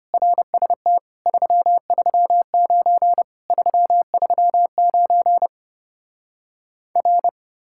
Morse code, heard as a keyed tone, about 33 wpm, 700 Hz. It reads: RST 339 339 R